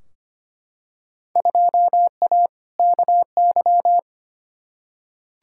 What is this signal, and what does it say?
Morse code, heard as a keyed tone, about 25 wpm, 700 Hz. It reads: 2A KY
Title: Field Day Exchanges - 25wpm